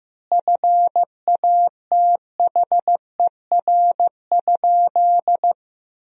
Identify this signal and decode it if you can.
Morse code, heard as a keyed tone, about 15 wpm, 700 Hz. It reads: FATHER?
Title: 2x Word-Spacing - Sentences from Top 300 Words - 15wpm